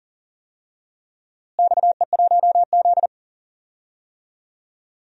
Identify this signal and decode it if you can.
Morse code, heard as a keyed tone, about 38 wpm, 700 Hz. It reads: XE1Z
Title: DXCC-100 Call Signs - 4 Characters - Repeated 2x - 40wpm